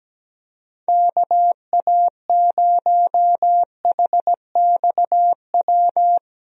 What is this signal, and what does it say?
Morse code, heard as a keyed tone, about 17 wpm, 700 Hz. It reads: KA0HXW
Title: USA Call Signs - 6 Characters - Repeated 2x - 17wpm